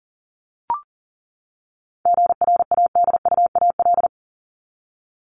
Morse code, heard as a keyed tone, about 40 wpm, 700 Hz. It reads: GRADUAL
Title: LY Suffix - Top 100 Words with Base Words 40wpm